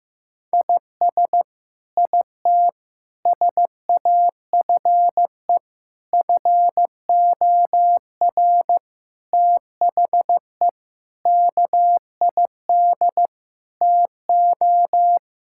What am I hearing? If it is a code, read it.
Morse code, heard as a keyed tone, about 15 wpm, 700 Hz. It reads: IS IT SAFE FOR THE KID TO
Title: Sentences from Top 1600 Words 15wpm